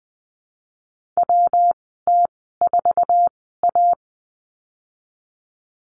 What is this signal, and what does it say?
Morse code, heard as a keyed tone, about 20 wpm, spaced as for 15 wpm, 700 Hz. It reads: WT4A
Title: 147 Lesson 24g - Letter y - Call Signs 20wpm